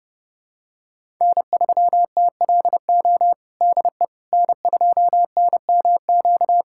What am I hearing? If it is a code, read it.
Morse code, heard as a keyed tone, about 30 wpm, 700 Hz. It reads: N3TLO DE N2NMQ
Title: USA Callsign DE Callsign - 5 Characters - 30wpm